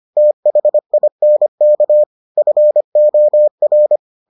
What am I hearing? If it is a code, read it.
Morse code, heard as a keyed tone, about 25 wpm, 600 Hz. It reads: THINK FOR